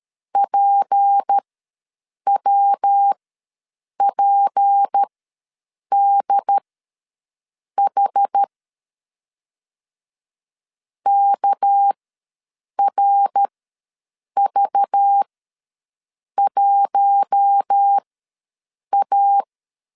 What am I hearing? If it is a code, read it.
Morse code, heard as a keyed tone, about 13 wpm, spaced as for 7 wpm, 800 Hz. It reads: PWPDH KRV1A
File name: LEZIONI / 6 PKX0 / 6 PK ws 12wpm 3116